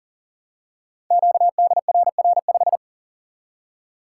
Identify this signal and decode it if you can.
Morse code, heard as a keyed tone, about 38 wpm, 700 Hz. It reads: QDRR5
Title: Code Groups - 5 Letter:Numbers - 40wpm